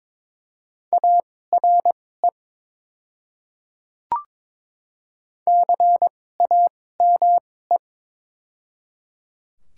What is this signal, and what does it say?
Morse code, heard as a keyed tone, about 21 wpm, spaced as for 16 wpm, 700 Hz. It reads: ARE CAME
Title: 2x Character Spacing - Top 200 Words - 22wpm